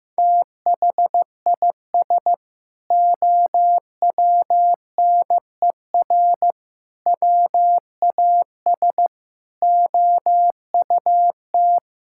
Morse code, heard as a keyed tone, about 16 wpm, 700 Hz. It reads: THIS OWNER WAS OUT
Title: Sentences from Top 1600 Words 15wpm